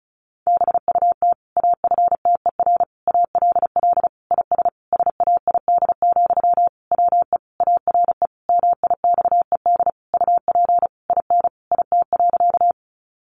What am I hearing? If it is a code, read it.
Morse code, heard as a keyed tone, about 34 wpm, 700 Hz. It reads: BUT AFTER ALL IS SAID, WE ARE MIXED UP IN IT.
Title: Sentences from Top 1500 Words 35wpm